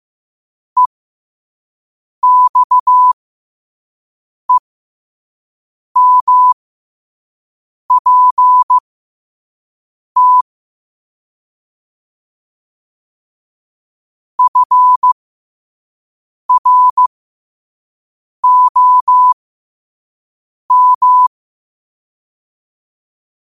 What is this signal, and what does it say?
Morse code, heard as a keyed tone, about 15 wpm, spaced as for 5 wpm, 1000 Hz. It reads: EXEMPT FROM
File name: qotdfiles / 5wpm